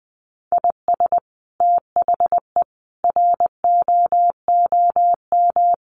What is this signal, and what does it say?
Morse code, heard as a keyed tone, about 20 wpm, 700 Hz. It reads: IS THE ROOM
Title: Sentences from Top 1400 Words 20wpm